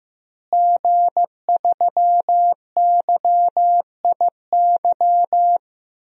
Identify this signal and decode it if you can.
Morse code, heard as a keyed tone, about 15 wpm, 700 Hz. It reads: G3YIY
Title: DXCC-100 Call Signs - 5 Characters - Repeated 2x - 15wpm